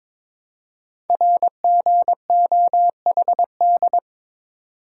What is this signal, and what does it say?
Morse code, heard as a keyed tone, about 22 wpm, 700 Hz. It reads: RGOHD
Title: Code Groups - 5 Letters - 22wpm